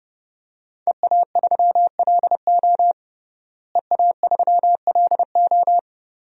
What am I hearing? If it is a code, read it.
Morse code, heard as a keyed tone, about 30 wpm, 700 Hz. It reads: EA3LO EA3LO
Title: DXCC-100 Call Signs - 5 Characters - Repeated 3x - 30wpm